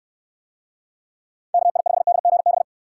Morse code, heard as a keyed tone, about 69 wpm, 700 Hz. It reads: KI5DGB